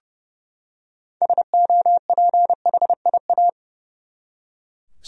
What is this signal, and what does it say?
Morse code, heard as a keyed tone, about 30 wpm, 700 Hz. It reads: SOPHIA
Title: USA Common Names - 30wpm